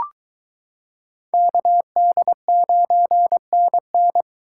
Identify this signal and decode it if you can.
Morse code, heard as a keyed tone, about 22 wpm, 700 Hz. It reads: KD9NN